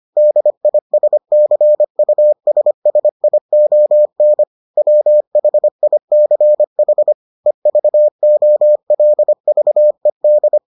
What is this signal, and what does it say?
Morse code, heard as a keyed tone, about 25 wpm, 600 Hz. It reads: DISCUSSION WHICH EVOLVED